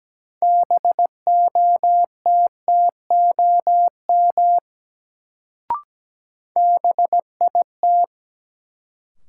Morse code, heard as a keyed tone, about 17 wpm, 700 Hz. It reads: BOTTOM BIT
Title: Sentences from Top 900 Words - New Word Drill - 17wpm